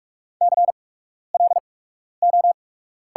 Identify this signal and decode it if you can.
Morse code, heard as a keyed tone, about 45 wpm, spaced as for 12 wpm, 690 Hz. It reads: CLO
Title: 8x Character Spacing - Top 500 Words - 45wpm